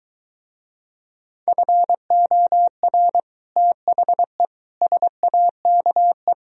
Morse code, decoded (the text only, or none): FOR THE SAKE